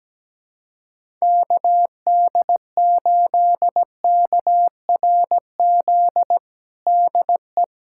Morse code, heard as a keyed tone, about 17 wpm, 700 Hz. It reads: KD8KRZ DE
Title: USA Callsign DE Callsign 17wpm